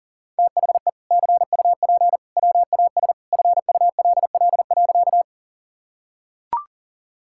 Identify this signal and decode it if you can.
Morse code, heard as a keyed tone, about 40 wpm, 700 Hz. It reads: THE CUP WAS FULL.
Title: Sentences from Top 1600 Words 40wpm